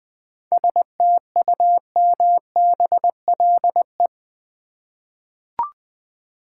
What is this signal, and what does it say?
Morse code, heard as a keyed tone, about 21 wpm, 700 Hz. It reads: STUMBLE
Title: 7 Letter Words - First 3 Letters Given - 20wpm